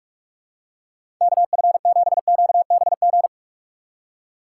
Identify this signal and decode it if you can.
Morse code, heard as a keyed tone, about 46 wpm, 700 Hz. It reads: KF7QBG
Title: USA Call Signs - 6 Characters - 6x Character Spacing - 45wpm